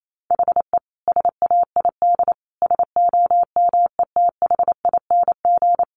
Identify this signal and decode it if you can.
Morse code, heard as a keyed tone, about 28 wpm, 700 Hz. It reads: HE SAID SOMETHING
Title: Sentences from Top 1500 Words 28wpm